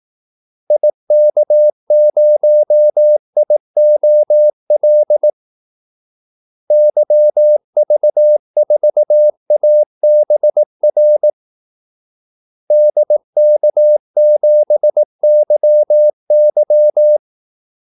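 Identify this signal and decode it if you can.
Morse code, heard as a keyed tone, about 18 wpm, 600 Hz. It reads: IK0IOL YV4ABR DK7YY
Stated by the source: Morse call-sign practice at 18 wpm.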